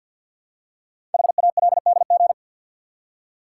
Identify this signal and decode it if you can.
Morse code, heard as a keyed tone, about 48 wpm, 700 Hz. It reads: HALDG
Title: Code Groups - 5 Letters - 50wpm